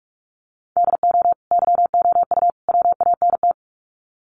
Morse code, heard as a keyed tone, about 44 wpm, 700 Hz. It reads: DO YOU WANT
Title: Sets of 3 Words 45wpm